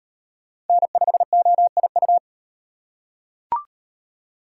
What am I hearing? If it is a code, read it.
Morse code, heard as a keyed tone, about 38 wpm, 700 Hz. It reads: N5OIU